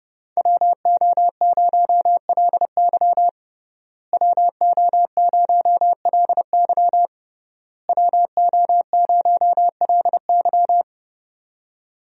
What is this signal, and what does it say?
Morse code, heard as a keyed tone, about 30 wpm, 700 Hz. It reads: WO0LY WO0LY WO0LY
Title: USA Call Signs - 5 Characters - Repeated 3x - 30wpm